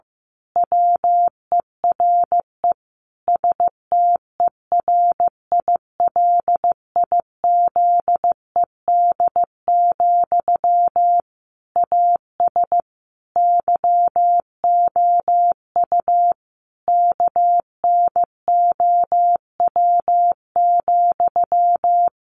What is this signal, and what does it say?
Morse code, heard as a keyed tone, about 15 wpm, 700 Hz. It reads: WERE STERILIZED, AS YOU KNOW,
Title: A Sound Of Thunder 15wpm